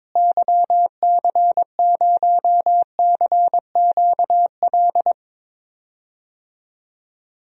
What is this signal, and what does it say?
Morse code, heard as a keyed tone, about 22 wpm, 700 Hz. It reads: YC0CQL
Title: DXCC-100 Call Signs - 6 Characters - Repeated 3x - 22wpm